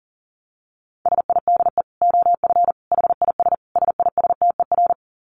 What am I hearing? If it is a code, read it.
Morse code, heard as a keyed tone, about 40 wpm, 700 Hz. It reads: SIDE OF HIS SISTER
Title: Dit Challenge 40wpm